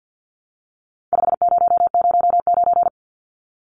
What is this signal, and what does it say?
Morse code, heard as a keyed tone, about 50 wpm, 700 Hz. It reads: 5009
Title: Numbers - 4-Digit 50wpm